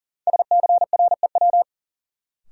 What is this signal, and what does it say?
Morse code, heard as a keyed tone, about 40 wpm, 700 Hz. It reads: SCREW
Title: Sentences from Top 1600 Words - New Word Drill - 40wpm